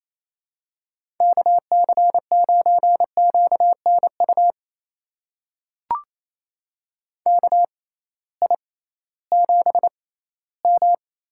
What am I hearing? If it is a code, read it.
Morse code, heard as a keyed tone, about 28 wpm, spaced as for 16 wpm, 700 Hz. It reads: KC9QNU KI7M
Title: USA Call Signs - 6 Characters - 6x Character Spacing - 28wpm